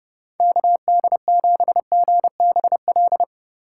Morse code, heard as a keyed tone, about 32 wpm, 700 Hz. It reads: KD7GBL
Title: USA Call Signs - 6 Characters - 3x Character Spacing - 30wpm